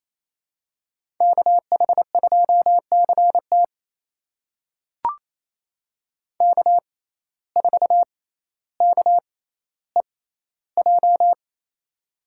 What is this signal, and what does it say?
Morse code, heard as a keyed tone, about 28 wpm, spaced as for 14 wpm, 700 Hz. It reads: KH2CT K4KEJ
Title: USA Call Signs - 5 Characters - 6x Character Spacing - 28wpm